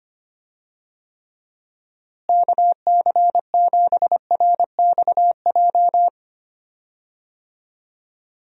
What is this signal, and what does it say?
Morse code, heard as a keyed tone, about 25 wpm, 700 Hz. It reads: KC7RXJ
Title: USA Call Signs - 6 Characters - Repeated 3x - 25wpm